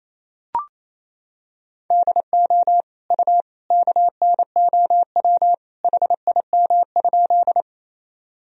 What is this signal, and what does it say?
Morse code, heard as a keyed tone, about 28 wpm, 700 Hz. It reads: DO U KNOW HIM?
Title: Sentences from Top 100 Words and Abbreviations 28wpm